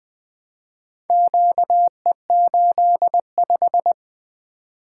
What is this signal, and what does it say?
Morse code, heard as a keyed tone, about 20 wpm, 700 Hz. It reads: QE85
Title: Four Letter-Number Combinations - 20wpm